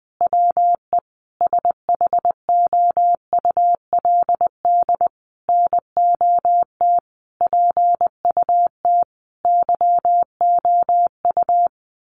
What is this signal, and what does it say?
Morse code, heard as a keyed tone, about 19 wpm, 700 Hz. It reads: WE SHOULD NOT PUT YOU
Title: Sentences from Top 1100 Words 20wpm